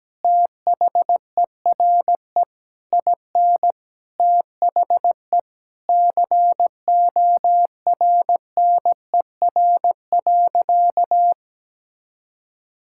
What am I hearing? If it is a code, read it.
Morse code, heard as a keyed tone, about 17 wpm, 700 Hz. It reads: THERE IN THE CORNER.
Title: Sentences from Top 1600 Words 17wpm